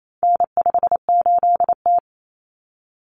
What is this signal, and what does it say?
Morse code, heard as a keyed tone, about 28 wpm, 700 Hz. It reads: N58T
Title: Common Cut Numbers - 4-Digit 28wpm